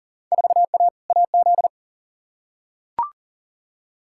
Morse code, heard as a keyed tone, about 40 wpm, 700 Hz. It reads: 4A AZ
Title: Field Day Exchanges - 40wpm